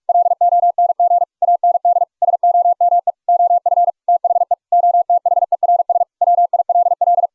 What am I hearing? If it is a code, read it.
Morse code, heard as a keyed tone, about 44 wpm, 700 Hz. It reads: LONG AND SOME OF THE OTHERS WILL